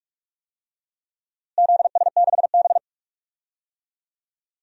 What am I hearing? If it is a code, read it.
Morse code, heard as a keyed tone, about 46 wpm, 700 Hz. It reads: ZS6B
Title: DXCC-100 Call Signs - 4 Characters - Repeated 3x - 45wpm